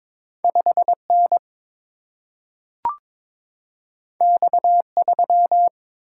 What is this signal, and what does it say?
Morse code, heard as a keyed tone, about 22 wpm, 700 Hz. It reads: 5 N X 3
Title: Single-Letter Single-Number 22wpm